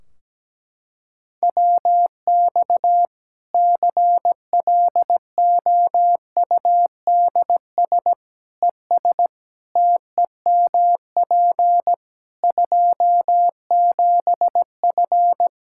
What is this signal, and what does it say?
Morse code, heard as a keyed tone, about 17 wpm, 700 Hz. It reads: WX CLOUDS ES TEMP 27F
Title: Practice QSO - Exchange 2 - Info - 17wpm